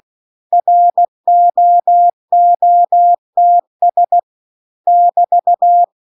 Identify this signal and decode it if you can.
Morse code, heard as a keyed tone, about 16 wpm, 700 Hz. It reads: ROOTS <BT>
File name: CW-The-Big-Twoheaded-River-2-16WPM